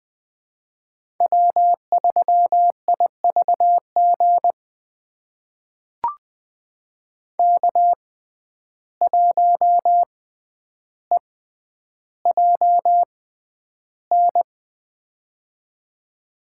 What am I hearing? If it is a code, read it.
Morse code, heard as a keyed tone, about 21 wpm, spaced as for 10 wpm, 700 Hz. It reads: W3IVG K1EJN
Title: USA Call Signs - 5 Characters - 6x Character Spacing - 20wpm